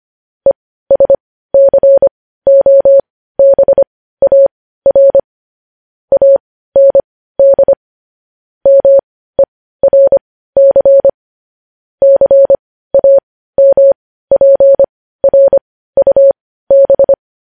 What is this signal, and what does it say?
Morse code, heard as a keyed tone, about 25 wpm, spaced as for 15 wpm, 550 Hz. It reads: ESCOBAR AND MERC CAMPRUB
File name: podcast / News headlines-1x-25x15